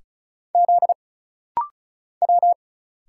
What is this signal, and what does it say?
Morse code, heard as a keyed tone, about 36 wpm, 700 Hz. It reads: Z W
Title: Single Letters - Rapid-Fire 35wpm